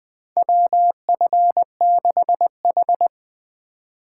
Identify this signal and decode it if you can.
Morse code, heard as a keyed tone, about 20 wpm, 700 Hz. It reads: WF6H
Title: USA Call Signs - 3x Character Spacing - 20wpm